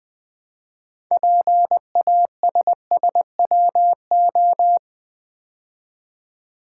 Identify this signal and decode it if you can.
Morse code, heard as a keyed tone, about 20 wpm, 700 Hz. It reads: PASSWO
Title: Build A Word - 3 Letters At A Time - 8-Letter Words - 20wpm